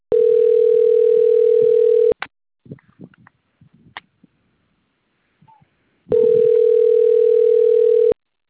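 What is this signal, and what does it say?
Ringback tone (North America), 440 + 480 Hz.
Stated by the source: Outbound phone call recorded on the Itatiba 3 phone system.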